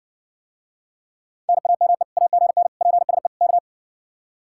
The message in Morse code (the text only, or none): NAME AGN PSE K